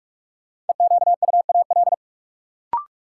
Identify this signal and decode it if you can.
Morse code, heard as a keyed tone, about 42 wpm, 700 Hz. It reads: EQUAL
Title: Sentences from Top 1400 Words - New Word Drill - 45wpm